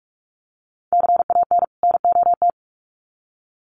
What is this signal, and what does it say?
Morse code, heard as a keyed tone, about 44 wpm, 700 Hz. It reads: CAN NOT
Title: Sets of 2 Words 45wpm